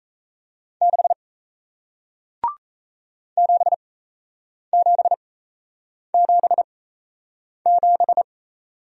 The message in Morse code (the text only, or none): 6 7777